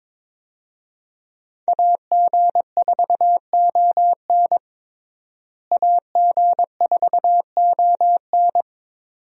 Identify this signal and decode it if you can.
Morse code, heard as a keyed tone, about 22 wpm, 700 Hz. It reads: AG4ON AG4ON
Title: USA Call Signs - 5 Characters - Repeated 2x - 22wpm